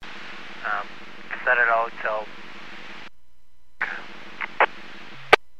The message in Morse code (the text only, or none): none